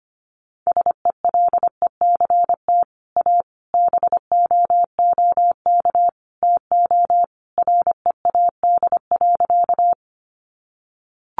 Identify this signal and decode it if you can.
Morse code, heard as a keyed tone, about 25 wpm, 700 Hz. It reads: SELECT A BOOK TO READ.
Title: Sentences from Top 1300 Words 25wpm